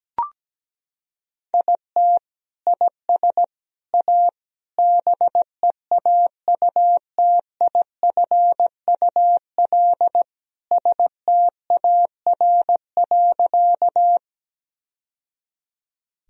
Morse code, heard as a keyed tone, about 17 wpm, 700 Hz. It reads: IT IS A BEAUTIFUL STAR.
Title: Sentences from Top 1000 Words - Review Part 2 - 17wpm